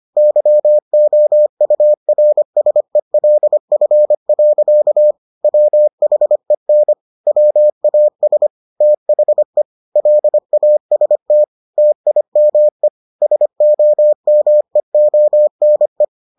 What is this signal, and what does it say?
Morse code, heard as a keyed tone, about 25 wpm, 600 Hz. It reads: YOURSELF. WHEN WAS THE LAST TIME SOMEONE